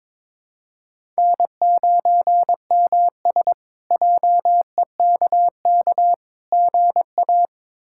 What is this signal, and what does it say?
Morse code, heard as a keyed tone, about 22 wpm, 700 Hz. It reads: N9MS JEKK GA
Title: K1USN SST QSO Practice - 22wpm